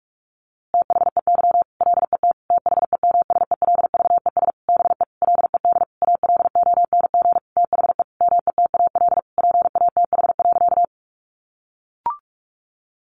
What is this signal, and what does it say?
Morse code, heard as a keyed tone, about 44 wpm, 700 Hz. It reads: THEY LET THEMSELVES BE LED ALONG THE METAL PATH.
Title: A Sound Of Thunder 45wpm